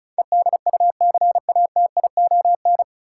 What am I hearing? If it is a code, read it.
Morse code, heard as a keyed tone, about 34 wpm, 700 Hz. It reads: EDUCATION
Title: Sentences from Top 1000 Words - Review Part 6 - New Word Drill - 35wpm